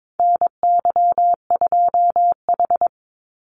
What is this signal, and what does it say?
Morse code, heard as a keyed tone, about 22 wpm, 700 Hz. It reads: NY2H
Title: Call Signs Repeated 2x - 22wpm